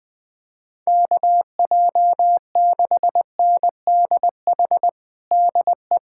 Morse code, heard as a keyed tone, about 20 wpm, 700 Hz. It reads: KJ6NDH DE
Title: USA Callsign DE Callsign 20wpm